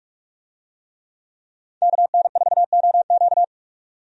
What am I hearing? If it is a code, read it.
Morse code, heard as a keyed tone, about 46 wpm, 700 Hz. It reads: KN4OQ
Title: USA Call Signs - 5 Characters - 45wpm